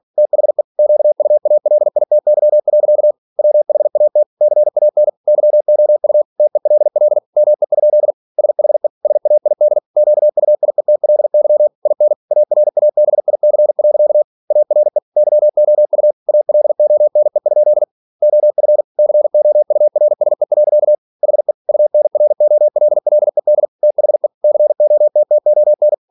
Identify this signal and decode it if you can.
Morse code, heard as a keyed tone, about 46 wpm, 600 Hz. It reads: THE QUALITY. WHAT CAN YOU TELL ME? SHE SAID QUIETLY IN ARABIC. ARE YOU ALONE? OF COURSE. HE UNROLLED THE COTTON
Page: Never 47wpm